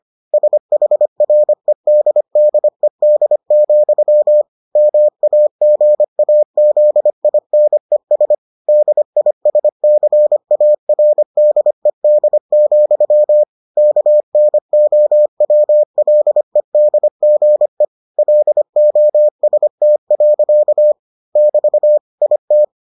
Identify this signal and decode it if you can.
Morse code, heard as a keyed tone, about 25 wpm, 600 Hz. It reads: SHREDDED, MAGAZINES DISCARDED, KNOWLEDGE LOST. <BT> IT